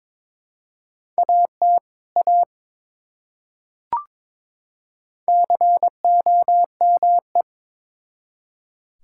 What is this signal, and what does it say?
Morse code, heard as a keyed tone, about 22 wpm, 700 Hz. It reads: AT A COME
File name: Build A Phrase - 2-Word Phrases - 22wpm